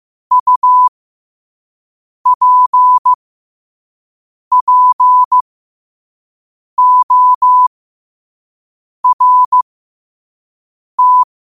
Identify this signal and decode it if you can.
Morse code, heard as a keyed tone, about 15 wpm, spaced as for 5 wpm, 1000 Hz. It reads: UPPORT
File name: qotdfiles / 5wpm